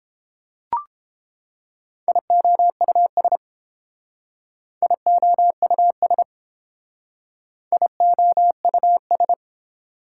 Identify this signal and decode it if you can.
Morse code, heard as a keyed tone, about 30 wpm, 700 Hz. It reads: IOUS IOUS IOUS